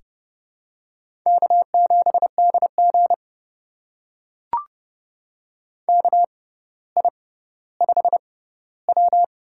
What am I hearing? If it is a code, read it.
Morse code, heard as a keyed tone, about 31 wpm, spaced as for 16 wpm, 700 Hz. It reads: K7DG KI5W
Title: USA Call Signs - 4 Characters - 6x Character Spacing - 30wpm